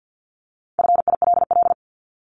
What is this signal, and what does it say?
Morse code, heard as a keyed tone, about 50 wpm, 700 Hz. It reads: FILL
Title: ED Suffix - Top 100 Words with Base Words 50wpm